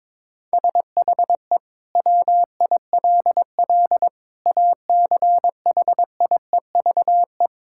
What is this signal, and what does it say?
Morse code, heard as a keyed tone, about 22 wpm, 700 Hz. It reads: SHE WILL ACHIEVE
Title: Sentences from Top 1000 Words - Review Part 6 - 22wpm